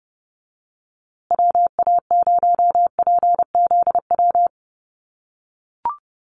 Morse code, heard as a keyed tone, about 30 wpm, 700 Hz. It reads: WA0PZW
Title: USA Call Signs Repeated 3x 30wpm